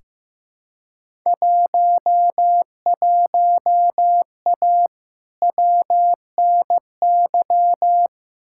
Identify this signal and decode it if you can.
Morse code, heard as a keyed tone, about 15 wpm, 700 Hz. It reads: 11A WNY
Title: Field Day Exchanges Encore - 15wpm